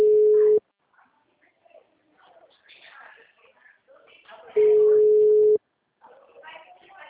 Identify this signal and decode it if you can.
Ringback tone (Europe, 425 Hz), with speech.